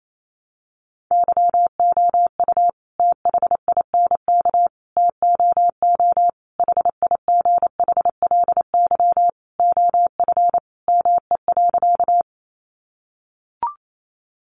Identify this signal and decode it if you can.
Morse code, heard as a keyed tone, about 28 wpm, 700 Hz. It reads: YOU THINK TOO HIGHLY OF ME.
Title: Sentences from Top 1100 Words 28wpm